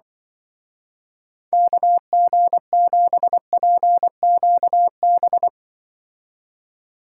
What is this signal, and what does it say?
Morse code, heard as a keyed tone, about 23 wpm, 700 Hz. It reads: KG7PQB
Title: Speed Racing - USA 2x3 Call Signs 20wpm